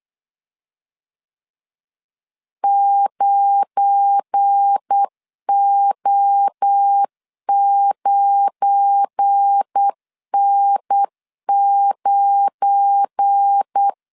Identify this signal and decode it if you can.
Morse code, heard as a keyed tone, about 8 wpm, 800 Hz. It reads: 9O9N9
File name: LEZIONI / 7 GF9Z / 7 GF9 8wpm 3545